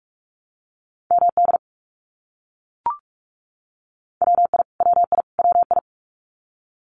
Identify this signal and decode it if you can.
Morse code, heard as a keyed tone, about 44 wpm, 700 Hz. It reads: MD WI WI WI
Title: US States Repeated 3x 45wpm